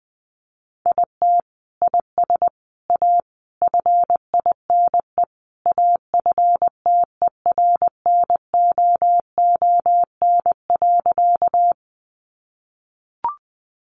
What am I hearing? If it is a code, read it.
Morse code, heard as a keyed tone, about 20 wpm, 700 Hz. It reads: IT IS A FINE AFTERNOON.
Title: Sentences from Top 1500 Words 20wpm